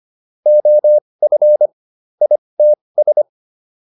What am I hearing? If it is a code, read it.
Morse code, heard as a keyed tone, about 25 wpm, spaced as for 20 wpm, 600 Hz. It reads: OF ITS